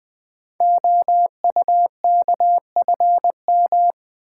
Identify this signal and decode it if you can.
Morse code, heard as a keyed tone, about 20 wpm, 700 Hz. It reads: OUKFM
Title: Code Groups - 5 Letters - 20wpm